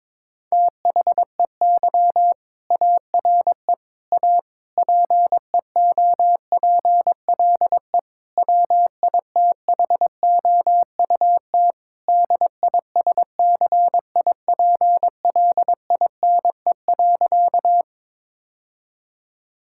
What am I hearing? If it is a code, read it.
Morse code, heard as a keyed tone, about 22 wpm, 700 Hz. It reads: THEY ARE A PEOPLE WITHOUT DISCIPLINE.
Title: Sentences from Top 1000 Words - Review Part 6 - 22wpm